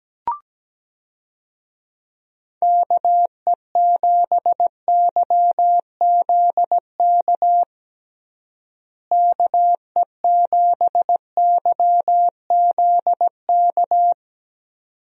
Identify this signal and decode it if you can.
Morse code, heard as a keyed tone, about 17 wpm, 700 Hz. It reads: KE7YZK KE7YZK
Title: USA Call Signs - 6 Characters - Repeated 2x - 17wpm